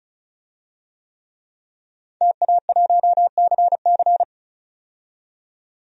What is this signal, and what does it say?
Morse code, heard as a keyed tone, about 36 wpm, 700 Hz. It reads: TA1CC
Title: DXCC-100 Call Signs - 5 Characters - 35wpm